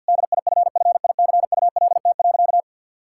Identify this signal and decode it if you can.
Morse code, heard as a keyed tone, about 48 wpm, 700 Hz. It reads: DIFFICULT.
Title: Sentences from Top 1600 Words 50wpm